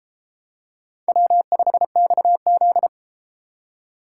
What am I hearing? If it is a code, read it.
Morse code, heard as a keyed tone, about 31 wpm, 700 Hz. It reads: W5XZ